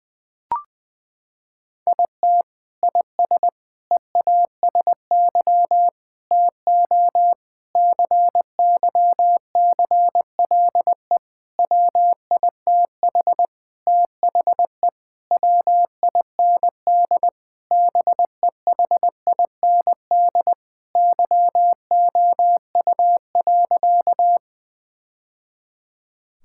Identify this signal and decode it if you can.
Morse code, heard as a keyed tone, about 21 wpm, 700 Hz. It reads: IT IS EASY TO CYCLE WITH THE WIND BEHIND YOU.
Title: Sentences from Top 1000 Words - Review Part 6 - 20wpm